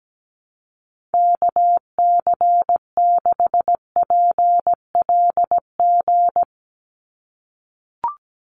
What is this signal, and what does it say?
Morse code, heard as a keyed tone, about 17 wpm, 700 Hz. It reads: KC6PLG
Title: USA Call Signs 17wpm